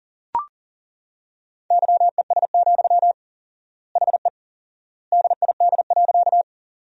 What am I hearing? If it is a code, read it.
Morse code, heard as a keyed tone, about 40 wpm, 700 Hz. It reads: YES, HE DID.
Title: 4x Word-Spacing - Sentences from Top 500 Words - 40wpm